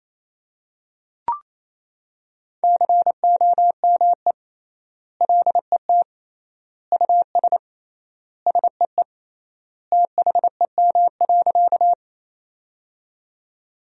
Morse code, heard as a keyed tone, about 28 wpm, 700 Hz. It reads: COME LET US SEE THEM.
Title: 3x Word-Spacing - Sentences from Top 300 Words - 28wpm